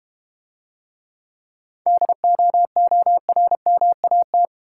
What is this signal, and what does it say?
Morse code, heard as a keyed tone, about 32 wpm, 700 Hz. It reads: DOORMAT